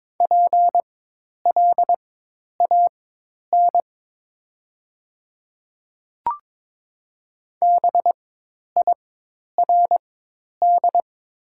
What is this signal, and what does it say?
Morse code, heard as a keyed tone, about 22 wpm, spaced as for 10 wpm, 700 Hz. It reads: PLAN BIRD
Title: 4x Character Spacing - Top 500 Words - 22wpm